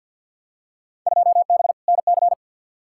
Morse code, heard as a keyed tone, about 52 wpm, 700 Hz. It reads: 2B NC
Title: Field Day Exchanges Encore - 50wpm